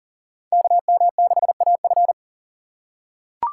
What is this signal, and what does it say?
Morse code, heard as a keyed tone, about 40 wpm, 700 Hz. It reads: KM6AF
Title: USA Call Signs - 5 Characters - 2x Character Spacing - 40wpm